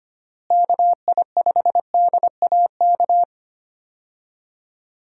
Morse code, heard as a keyed tone, about 25 wpm, 700 Hz. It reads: KI5DAK
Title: USA Call Signs - 6 Characters - Repeated 3x - 25wpm